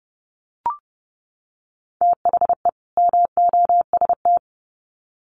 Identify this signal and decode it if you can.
Morse code, heard as a keyed tone, about 30 wpm, 700 Hz. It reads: THE MOST
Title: Sets of 2 Words 30wpm